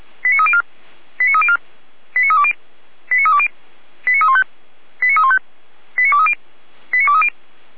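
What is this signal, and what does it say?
Five-tone selective call (ZVEI), heard as quick runs of five tones, tones of about 70 ms. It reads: 89304 89304 89320 89320 89315 89315 89230 89230
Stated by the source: ZVEI five-tone selective call received by SDR.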